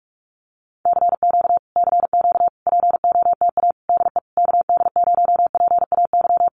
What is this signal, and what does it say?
Morse code, heard as a keyed tone, about 44 wpm, 700 Hz. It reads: CQ CQ POTA DE KD0PAY
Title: Common QSO Phrases Encore 45wpm